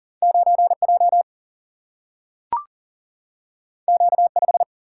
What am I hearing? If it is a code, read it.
Morse code, heard as a keyed tone, about 40 wpm, 700 Hz. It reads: 9J Q5
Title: Single-Letter Single-Number 40wpm